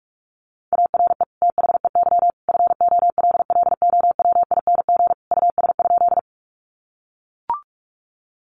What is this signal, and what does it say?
Morse code, heard as a keyed tone, about 44 wpm, 700 Hz. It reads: ARE THEY FOLLOWING US?
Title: Sentences from Top 700 Words 45wpm